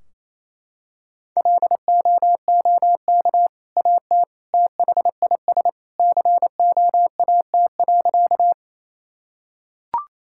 Morse code, heard as a keyed tone, about 28 wpm, 700 Hz. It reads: LOOK AT THIS COAT.
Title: Sentences from Top 1600 Words 28wpm